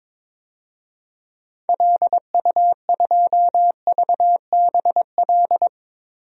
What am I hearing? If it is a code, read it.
Morse code, heard as a keyed tone, about 22 wpm, 700 Hz. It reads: LU2VBL
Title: DXCC-100 Call Signs - 6 Characters - 22wpm